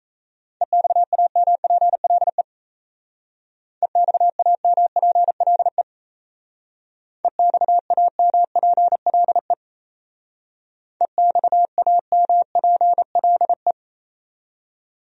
Morse code, heard as a keyed tone, about 33 wpm, 700 Hz. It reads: EXAMPLE EXAMPLE EXAMPLE EXAMPLE
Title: Speed-Racing Top 300 Words 28wpm